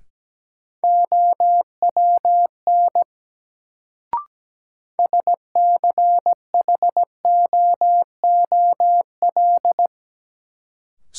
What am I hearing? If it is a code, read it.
Morse code, heard as a keyed tone, about 17 wpm, 700 Hz. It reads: OWN SCHOOL
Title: Sentences from Top 300 Words - New Word Drill - 17wpm